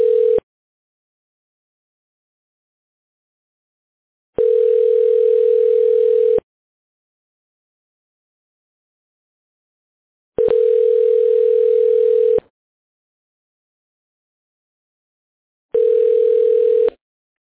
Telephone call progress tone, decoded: ringback tone